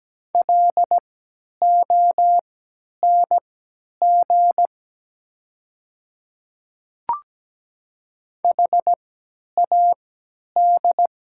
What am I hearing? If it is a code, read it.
Morse code, heard as a keyed tone, about 17 wpm, spaced as for 10 wpm, 690 Hz. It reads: LONG HAD
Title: 3x Character Spacing - Top 100 Words - 17wpm